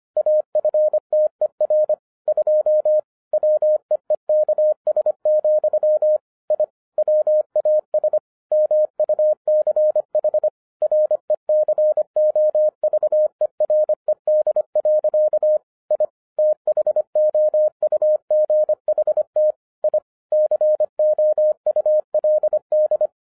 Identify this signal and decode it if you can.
Morse code, heard as a keyed tone, about 25 wpm, 610 Hz. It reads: AFTER 2 WEEKS, I WAS MUCH RECOVERED. I THOUGHT I COULD